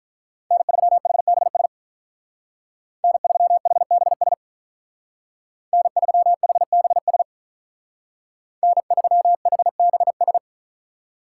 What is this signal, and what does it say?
Morse code, heard as a keyed tone, about 46 wpm, 700 Hz. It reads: N3HBS N3HBS N3HBS N3HBS